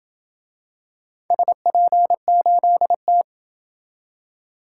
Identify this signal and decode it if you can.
Morse code, heard as a keyed tone, about 26 wpm, 700 Hz. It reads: SP8T